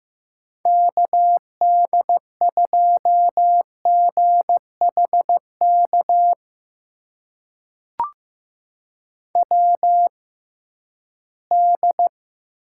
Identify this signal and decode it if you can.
Morse code, heard as a keyed tone, about 15 wpm, spaced as for 11 wpm, 700 Hz. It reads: KD2GHK WD
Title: USA Call Signs - 6 Characters - 6x Character Spacing - 15wpm